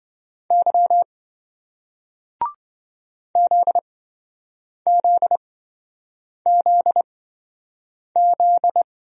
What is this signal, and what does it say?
Morse code, heard as a keyed tone, about 27 wpm, spaced as for 7 wpm, 700 Hz. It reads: Y ZZZZ